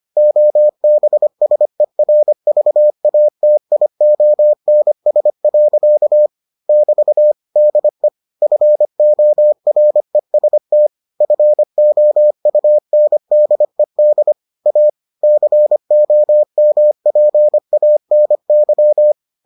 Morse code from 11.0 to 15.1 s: FOUNDED A

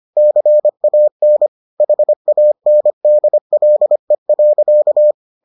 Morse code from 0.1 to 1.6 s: CAN

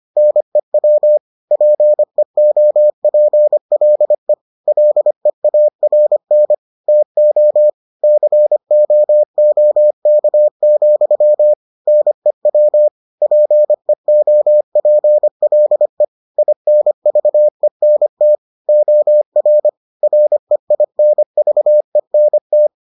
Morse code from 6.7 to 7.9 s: TO

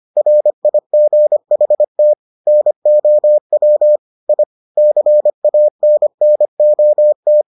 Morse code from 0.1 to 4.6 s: RIGHT NOW I

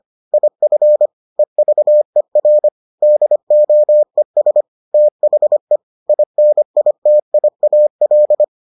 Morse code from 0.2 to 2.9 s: IF EVER

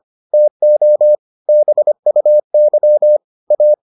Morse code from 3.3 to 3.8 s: A

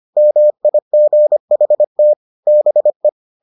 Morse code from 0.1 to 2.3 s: MIGHT

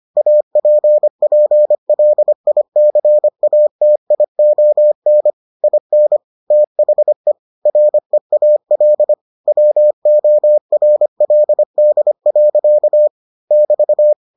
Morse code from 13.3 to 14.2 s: <BT>

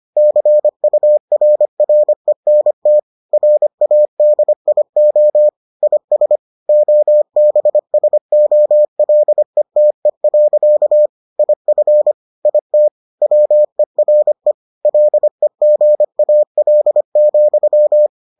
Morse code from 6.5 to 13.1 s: OBSOLETE. IF IT

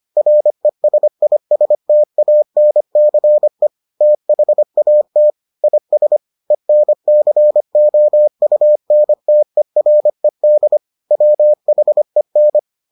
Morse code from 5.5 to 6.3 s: IS

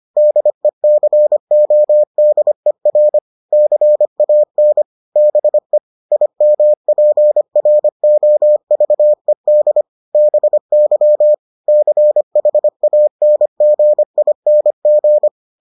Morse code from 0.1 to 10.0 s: DECODER CAN BE IMPROVED

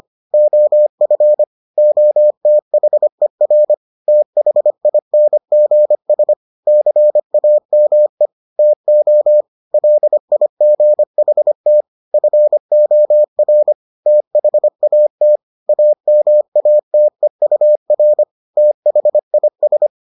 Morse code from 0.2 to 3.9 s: OF OTHER

